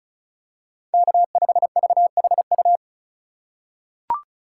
Morse code, heard as a keyed tone, about 36 wpm, 700 Hz. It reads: K5VHU
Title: USA Call Signs - 5 Characters - 4x Character Spacing - 35wpm